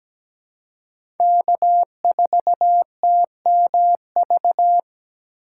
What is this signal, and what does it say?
Morse code, heard as a keyed tone, about 17 wpm, 700 Hz. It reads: K4TMV
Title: USA Call Signs - 5 Characters - 3x Character Spacing - 17wpm